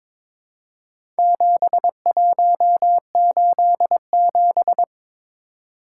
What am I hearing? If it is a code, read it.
Morse code, heard as a keyed tone, about 22 wpm, 700 Hz. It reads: 7187
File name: practice / 22wpm / Four Letter-Number Combinations - 22wpm